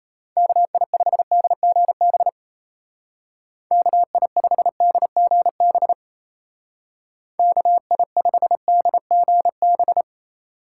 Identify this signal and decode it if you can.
Morse code, heard as a keyed tone, about 34 wpm, 700 Hz. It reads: KI5DGB KI5DGB KI5DGB
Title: Speed Racing - USA 2x3 Call Signs 28wpm